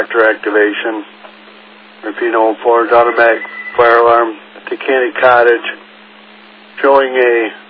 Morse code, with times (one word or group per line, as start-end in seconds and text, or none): none